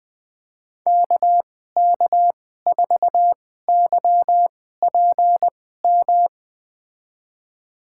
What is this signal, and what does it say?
Morse code, heard as a keyed tone, about 20 wpm, spaced as for 15 wpm, 700 Hz. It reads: KK4YPM
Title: USA Call Signs - 6 Characters - 2x Character Spacing - 20wpm